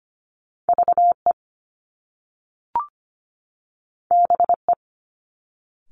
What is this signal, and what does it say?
Morse code, heard as a keyed tone, about 25 wpm, 700 Hz. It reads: VE BE
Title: Top Two Letter Combinations in Words 25wpm